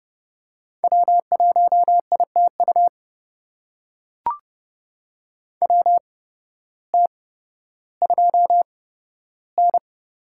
Morse code, heard as a keyed tone, about 30 wpm, spaced as for 14 wpm, 700 Hz. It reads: W1ITU WT2N
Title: USA Call Signs - 5 Characters - 8x Character Spacing - 30wpm